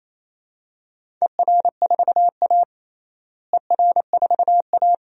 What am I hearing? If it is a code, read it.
Morse code, heard as a keyed tone, about 28 wpm, 700 Hz. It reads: ER4A ER4A
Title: DXCC-100 Call Signs - 4 Characters - Repeated 2x - 28wpm